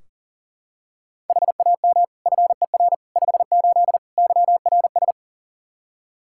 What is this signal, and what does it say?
Morse code, heard as a keyed tone, about 40 wpm, 700 Hz. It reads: HAM FER 58 YRS
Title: Practice QSO - Exchange 3 - More Info - 40wpm